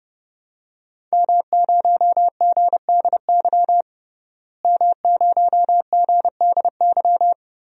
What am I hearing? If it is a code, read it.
Morse code, heard as a keyed tone, about 30 wpm, 700 Hz. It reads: M0GDY M0GDY
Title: DXCC-100 Call Signs - 5 Characters - Repeated 2x - 30wpm